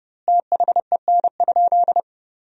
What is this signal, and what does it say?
Morse code, heard as a keyed tone, about 32 wpm, 700 Hz. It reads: THEN?
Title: 2x Word-Spacing - Sentences from Top 200 Words - 30wpm